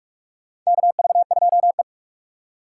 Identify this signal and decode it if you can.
Morse code, heard as a keyed tone, about 44 wpm, 690 Hz. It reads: KV2E
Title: USA Call Signs - 4 Characters - 8x Character Spacing - 45wpm